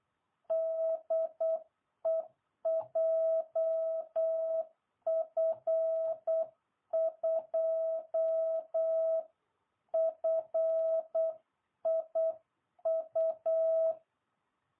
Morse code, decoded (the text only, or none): DE JF2FIU